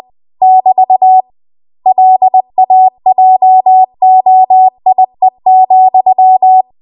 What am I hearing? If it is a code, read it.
Morse code, heard as a keyed tone, about 20 wpm, 750 Hz. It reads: <BT> LAJOIE,
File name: Archive / 20 WPM / 20WPM